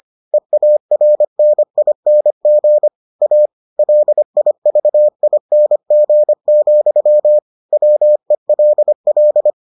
Morse code, heard as a keyed tone, about 25 wpm, 600 Hz. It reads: EARNING A LIVING, WELL